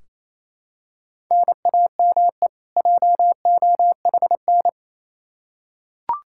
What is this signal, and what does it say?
Morse code, heard as a keyed tone, about 27 wpm, 700 Hz. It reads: NAME JOHN
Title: Practice QSO - Exchange 1 - Beginning - 28wpm